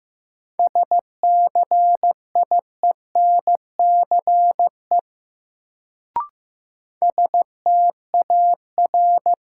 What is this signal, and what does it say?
Morse code, heard as a keyed tone, about 15 wpm, 700 Hz. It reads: SCIENCE STAR
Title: Sentences from Top 1000 Words - Review Part 2 - New Word Drill - 15wpm